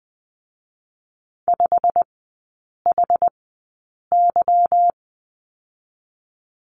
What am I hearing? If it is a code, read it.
Morse code, heard as a keyed tone, about 20 wpm, spaced as for 8 wpm, 690 Hz. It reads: 5HY
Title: 172 Lesson 28e - Number 9 - Three Characters 20wpm